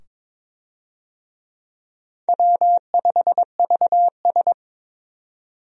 Morse code, heard as a keyed tone, about 22 wpm, 700 Hz. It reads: W5VS
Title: USA Call Signs - 4 Characters - 22wpm